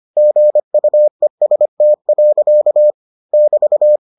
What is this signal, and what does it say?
Morse code, heard as a keyed tone, about 25 wpm, 600 Hz. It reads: GUEST. <BT>